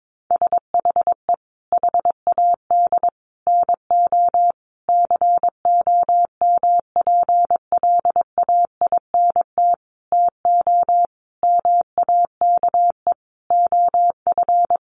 Morse code, heard as a keyed tone, about 22 wpm, 700 Hz. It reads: SHE HAD NO COMPLAINT TO MAKE OF
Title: Sentences from Top 1400 Words 22wpm